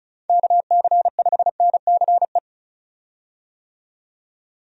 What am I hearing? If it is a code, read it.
Morse code, heard as a keyed tone, about 34 wpm, 700 Hz. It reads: KC5NCE
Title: USA Call Signs - 6 Characters - 35wpm